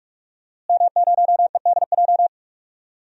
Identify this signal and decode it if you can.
Morse code, heard as a keyed tone, about 41 wpm, 700 Hz. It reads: M0EDJ
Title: DXCC-100 Call Signs - 5 Characters - Repeated 3x - 45wpm